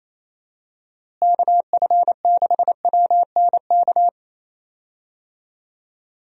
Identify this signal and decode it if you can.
Morse code, heard as a keyed tone, about 28 wpm, 700 Hz. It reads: KF6WNK
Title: USA Call Signs - 6 Characters - Repeated 3x - 28wpm